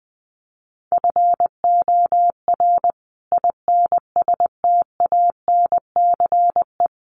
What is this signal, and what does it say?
Morse code, heard as a keyed tone, about 20 wpm, 700 Hz. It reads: FOR INSTANCE
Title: Sentences from Top 800 Words 20wpm